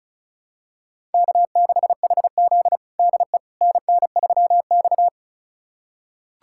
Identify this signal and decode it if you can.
Morse code, heard as a keyed tone, about 34 wpm, 700 Hz. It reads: K6HZ DE NN3X
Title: USA Callsign DE Callsign - 4 Characters - 35wpm